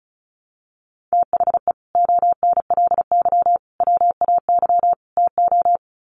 Morse code, heard as a keyed tone, about 34 wpm, 700 Hz. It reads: THE ONLY WAY TO
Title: Sets of 4 Words 35wpm